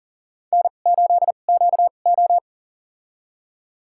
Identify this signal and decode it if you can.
Morse code, heard as a keyed tone, about 40 wpm, spaced as for 29 wpm, 700 Hz. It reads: N8QO
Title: USA Call Signs - 4 Characters - 2x Character Spacing - 40wpm